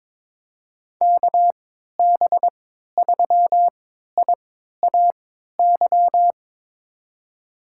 Morse code, heard as a keyed tone, about 22 wpm, spaced as for 12 wpm, 700 Hz. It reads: KB3IAY
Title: USA Call Signs - 3x Character Spacing - 22wpm